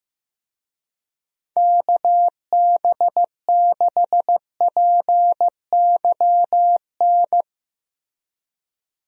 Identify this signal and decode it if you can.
Morse code, heard as a keyed tone, about 15 wpm, 700 Hz. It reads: KB6PYN
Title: USA Call Signs - 6 Characters - Repeated 2x - 15wpm